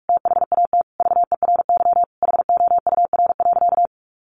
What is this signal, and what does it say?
Morse code, heard as a keyed tone, about 44 wpm, 700 Hz. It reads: THAT VERY HOUR.
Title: Sentences from Top 1500 Words 45wpm